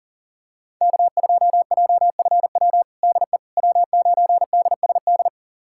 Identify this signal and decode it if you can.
Morse code, heard as a keyed tone, about 40 wpm, 700 Hz. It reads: K2JFW DE W9DSD
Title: USA Callsign DE Callsign - 5 Characters - 40wpm